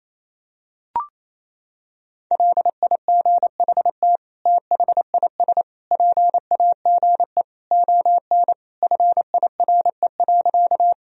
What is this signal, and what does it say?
Morse code, heard as a keyed tone, about 28 wpm, 700 Hz. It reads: LIGHT THIS PAGE ON FIRE.
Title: Sentences from Top 1000 Words - Review Part 1 - 28wpm